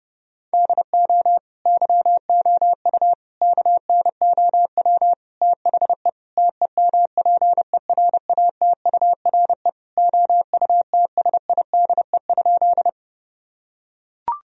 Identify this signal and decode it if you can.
Morse code, heard as a keyed tone, about 30 wpm, 700 Hz. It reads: DO YOU KNOW THE TEMPERATURE OUTSIDE?
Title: Sentences from Top 1000 Words - Review Part 4 - 30wpm